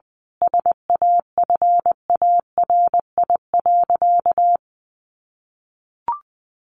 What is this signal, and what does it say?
Morse code, heard as a keyed tone, about 20 wpm, 700 Hz. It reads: SAFARI.
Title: A Sound Of Thunder 20wpm